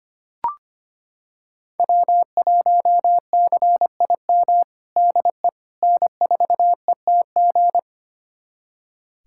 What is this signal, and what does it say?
Morse code, heard as a keyed tone, about 25 wpm, 700 Hz. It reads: W1CIM DE N4ETG
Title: USA Callsign DE Callsign - 5 Characters - 25wpm